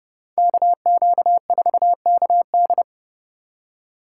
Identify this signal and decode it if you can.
Morse code, heard as a keyed tone, about 29 wpm, 700 Hz. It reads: KQ4KD